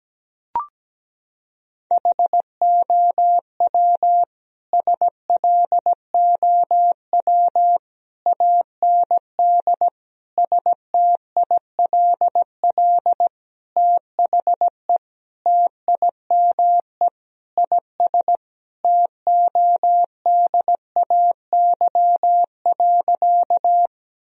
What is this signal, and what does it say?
Morse code, heard as a keyed tone, about 17 wpm, 700 Hz. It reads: HOW SLOW AND STILL THE TIME IS TODAY.
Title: Sentences from Top 1000 Words - Review Part 2 - 17wpm